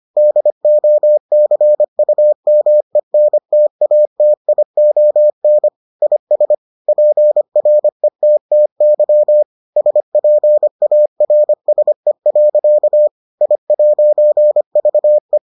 Morse code, heard as a keyed tone, about 25 wpm, 600 Hz. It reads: DOCUMENTATION IS PRETTY SPARSE. I'VE